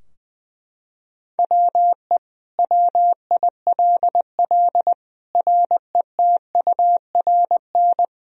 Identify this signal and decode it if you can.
Morse code, heard as a keyed tone, about 20 wpm, 700 Hz. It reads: WE WILL RETURN
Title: Sentences from Top 1000 Words - Review Part 3 - 20wpm